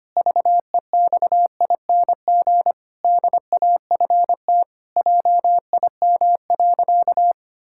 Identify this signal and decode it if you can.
Morse code, heard as a keyed tone, about 25 wpm, 700 Hz. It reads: VEXING DAFT JIM.
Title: Pangram Challenge v2 - 25wpm